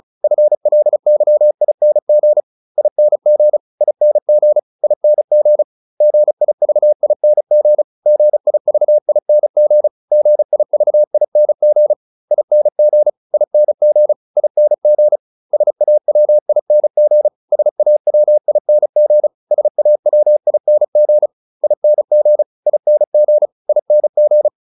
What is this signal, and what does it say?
Morse code, heard as a keyed tone, about 34 wpm, 600 Hz. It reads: FLYING ING ING ING GIVING GIVING GIVING ING ING ING SAWING SAWING SAWING ING ING ING